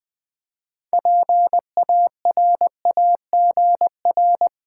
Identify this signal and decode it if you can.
Morse code, heard as a keyed tone, about 20 wpm, 700 Hz. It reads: PARAGR
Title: Build A Word - 3 Letters At A Time - 9-Letter Words - 20wpm